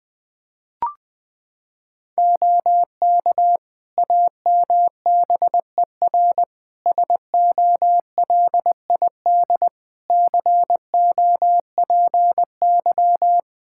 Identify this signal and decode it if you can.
Morse code, heard as a keyed tone, about 20 wpm, 700 Hz. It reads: OK AMBER SOLID COPY
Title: Practice QSO - Exchange 3 - More Info - 20wpm